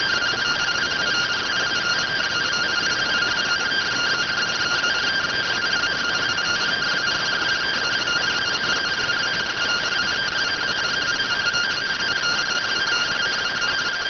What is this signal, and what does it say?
Two-tone teleprinter signal, about 170 Hz shift.